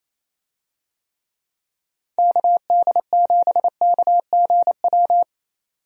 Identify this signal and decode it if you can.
Morse code, heard as a keyed tone, about 28 wpm, 700 Hz. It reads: KD7KGW
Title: USA Call Signs - 6 Characters - 28wpm